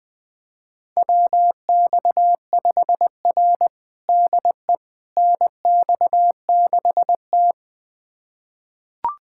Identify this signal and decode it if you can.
Morse code, heard as a keyed tone, about 21 wpm, 700 Hz. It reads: WX5R DE NX6T
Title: USA Callsign DE Callsign - 4 Characters - 20wpm